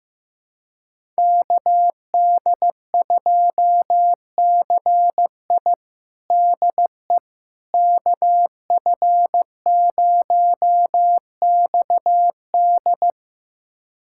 Morse code, heard as a keyed tone, about 16 wpm, 700 Hz. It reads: KD2CI DE KF0XD
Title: USA Callsign DE Callsign - 5 Characters - 15wpm